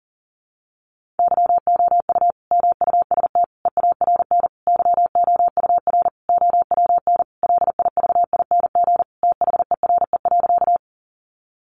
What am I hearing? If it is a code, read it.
Morse code, heard as a keyed tone, about 40 wpm, 700 Hz. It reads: YOU MUST EARN YOUR OWN LIVING THERE.
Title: Sentences from Top 1500 Words 40wpm